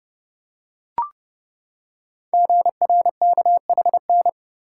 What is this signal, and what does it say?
Morse code, heard as a keyed tone, about 30 wpm, 700 Hz. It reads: GRKHN
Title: Code Groups - 5 Letter:Numbers - 30wpm